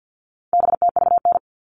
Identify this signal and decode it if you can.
Morse code, heard as a keyed tone, about 48 wpm, 700 Hz. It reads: BTVN
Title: All Cut Numbers - 4-Digit 50wpm